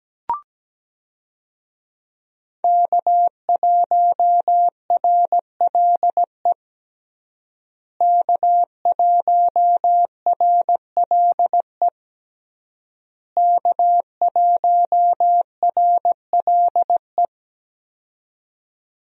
Morse code, heard as a keyed tone, about 17 wpm, 700 Hz. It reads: K1RLE K1RLE K1RLE
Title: USA Call Signs - 5 Characters - Repeated 3x - 17wpm